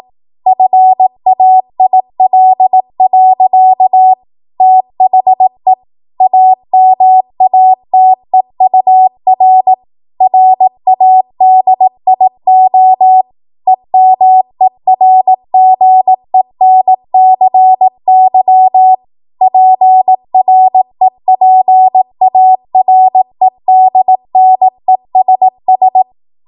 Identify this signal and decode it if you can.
Morse code, heard as a keyed tone, about 18 wpm, 750 Hz. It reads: FAIL. THE AMATEUR RADIO EMERGENCY PREPAREDNESS
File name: Morse / 18WPM